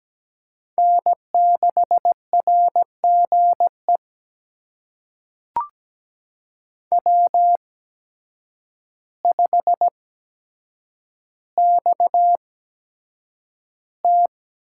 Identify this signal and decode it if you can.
Morse code, heard as a keyed tone, about 17 wpm, spaced as for 8 wpm, 700 Hz. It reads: N6RGE W5XT
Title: USA Call Signs - 5 Characters - 8x Character Spacing - 17wpm